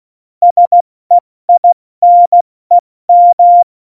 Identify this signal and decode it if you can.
Morse code, heard as a keyed tone, about 16 wpm, 700 Hz. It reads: SEINEM
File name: Morsekurs / Witze / Witz 08. Dezember 2020, 70 F80